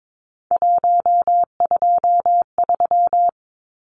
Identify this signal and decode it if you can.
Morse code, heard as a keyed tone, about 22 wpm, 700 Hz. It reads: 123
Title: Numbers - 3-Digit 22wpm